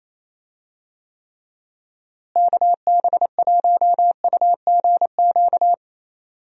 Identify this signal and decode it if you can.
Morse code, heard as a keyed tone, about 28 wpm, 700 Hz. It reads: KB1UGQ
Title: USA Call Signs - 6 Characters - Repeated 3x - 28wpm